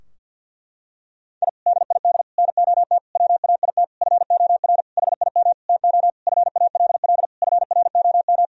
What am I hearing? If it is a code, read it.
Morse code, heard as a keyed tone, about 46 wpm, 700 Hz. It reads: I DID NOT WAIT FOR HIM TO FALL FROM
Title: Sentences from Top 1000 Words - Review Part 2 - 50wpm